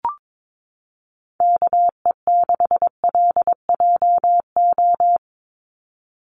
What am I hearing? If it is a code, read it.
Morse code, heard as a keyed tone, about 22 wpm, 700 Hz. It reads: KE6LJO
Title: USA Call Signs 22wpm